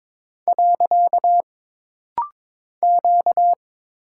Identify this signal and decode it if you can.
Morse code, heard as a keyed tone, about 22 wpm, 700 Hz. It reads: . Q
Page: This rapid-fire timing is geared to push your ICR (Instant Character Recognition) skills.